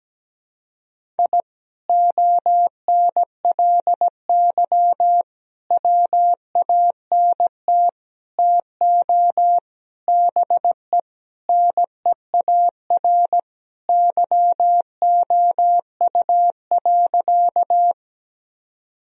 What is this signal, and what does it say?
Morse code, heard as a keyed tone, about 17 wpm, 700 Hz. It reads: I ONLY WANT TO BE NEAR YOU.
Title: Sentences from Top 1000 Words - Review Part 1 - 17wpm